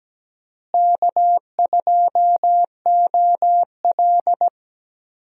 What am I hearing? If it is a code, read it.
Morse code, heard as a keyed tone, about 17 wpm, 700 Hz. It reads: K2OL
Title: USA Call Signs - 4 Characters - Repeated 3x - 17wpm